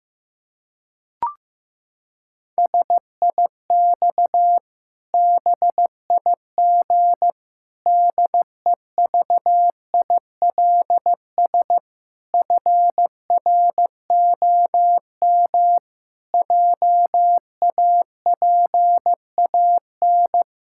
Morse code, heard as a keyed tone, about 16 wpm, 700 Hz. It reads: SIX BIG DEVILS FROM JAPAN
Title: Pangram Challenge v2 - 15wpm